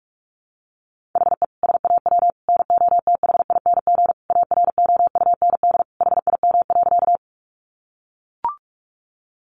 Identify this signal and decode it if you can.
Morse code, heard as a keyed tone, about 44 wpm, 700 Hz. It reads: HE SAW NOTHING AROUND HIM.
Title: Sentences from Top 600 Words 45wpm